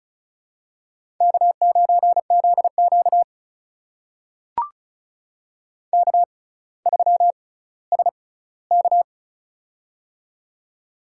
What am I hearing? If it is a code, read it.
Morse code, heard as a keyed tone, about 36 wpm, spaced as for 18 wpm, 700 Hz. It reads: K9ZQ K3SK
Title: USA Call Signs - 4 Characters - 6x Character Spacing - 35wpm